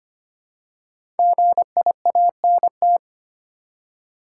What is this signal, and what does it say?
Morse code, heard as a keyed tone, about 25 wpm, 700 Hz. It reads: GIANT
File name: 5 Letter Words - First 3 Letters Given - 25wpm